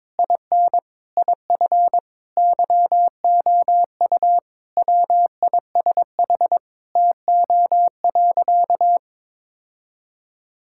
Morse code, heard as a keyed tone, about 22 wpm, 700 Hz. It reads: IN IF YOU WISH TO.
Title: Sentences from Top 1600 Words 22wpm